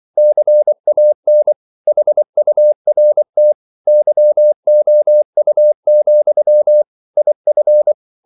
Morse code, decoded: CAN HURT YOU, IF